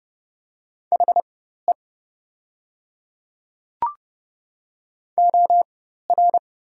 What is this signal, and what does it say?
Morse code, heard as a keyed tone, about 31 wpm, spaced as for 14 wpm, 700 Hz. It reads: HE OR
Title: 4x Character Spacing - Top 100 Words - 30wpm